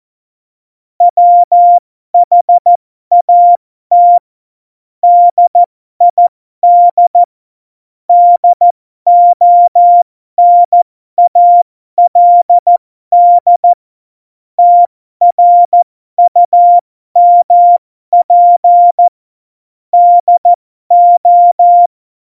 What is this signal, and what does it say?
Morse code, heard as a keyed tone, about 14 wpm, spaced as for 12 wpm, 700 Hz. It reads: WHAT DID DONALD TRUMP DO
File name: Morsekurs / Witze